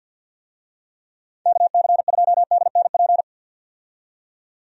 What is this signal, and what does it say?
Morse code, heard as a keyed tone, about 53 wpm, 700 Hz. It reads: KC2DNP
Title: USA Call Signs - 6 Characters - 50wpm